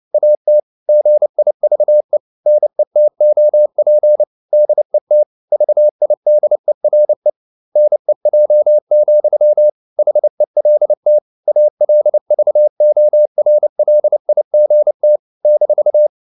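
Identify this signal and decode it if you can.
Morse code, heard as a keyed tone, about 29 wpm, 600 Hz. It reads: AT GIVE NETOP DET VIDERE NEJ, HELT ALVORLIGT -
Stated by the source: Fast (QRQ) Morse at 29 wpm, from CW med Gnister.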